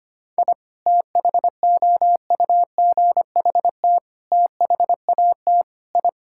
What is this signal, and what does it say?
Morse code, heard as a keyed tone, about 25 wpm, 700 Hz. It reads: I THOUGHT THAT I
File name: practice / 25wpm / Sentences with 3-Word Phrases - 25wpm